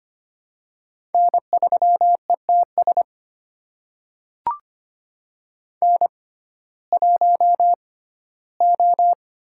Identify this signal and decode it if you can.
Morse code, heard as a keyed tone, about 25 wpm, spaced as for 15 wpm, 700 Hz. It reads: N3ETS N1O
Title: USA Call Signs - 5 Characters - 6x Character Spacing - 25wpm